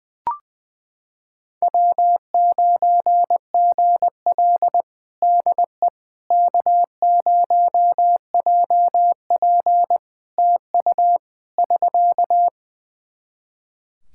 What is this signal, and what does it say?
Morse code, heard as a keyed tone, about 20 wpm, 700 Hz. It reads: W9GL DE K0JP TU <SK>
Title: Practice QSO - Exchange 4 - Ending - 20wpm